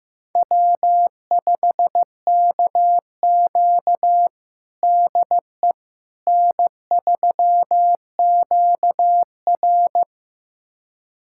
Morse code, heard as a keyed tone, about 15 wpm, 700 Hz. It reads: W5KQ DE N3QR
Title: USA Callsign DE Callsign - 4 Characters - 15wpm